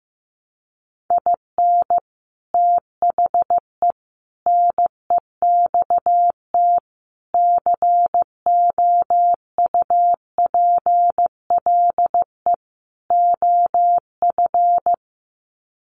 Morse code, heard as a keyed tone, about 15 wpm, 700 Hz. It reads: IN THE NEXT COUPLE OF
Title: Sets of 5 Words Encore2 15wpm